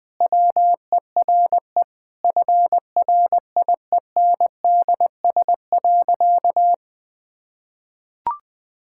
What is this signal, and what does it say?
Morse code, heard as a keyed tone, about 21 wpm, 700 Hz. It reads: WERE FRIENDS.
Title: Sentences with 2-Word Phrases - Spoken Beginning - 20wpm